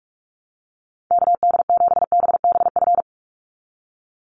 Morse code, heard as a keyed tone, about 44 wpm, 700 Hz. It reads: KD7BBF
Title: USA Call Signs 45wpm